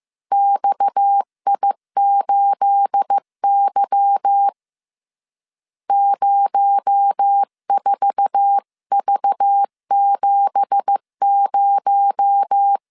Morse code, heard as a keyed tone, about 15 wpm, 800 Hz. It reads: XI8Y 04V70
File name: CW / DET / 1 9 1500C S C 14wpm 3245